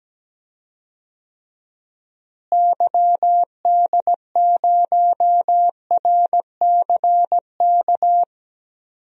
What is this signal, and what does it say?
Morse code, heard as a keyed tone, about 17 wpm, 700 Hz. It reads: YD0RCK
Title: DXCC-100 Call Signs - 6 Characters - Repeated 3x - 17wpm